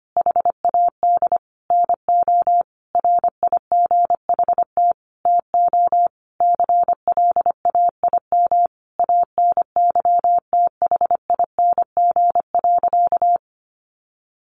Morse code, heard as a keyed tone, about 25 wpm, 700 Hz. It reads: HAD NO RIGHT TO CLAIM ANYTHING.
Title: Sentences from Top 1500 Words 25wpm